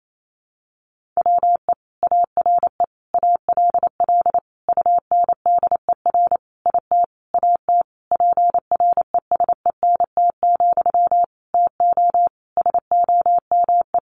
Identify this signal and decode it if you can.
Morse code, heard as a keyed tone, about 28 wpm, 700 Hz. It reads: WE ARE ALL UNDER IT AT PRESENT, TO SOME
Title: Sentences from Top 1500 Words 28wpm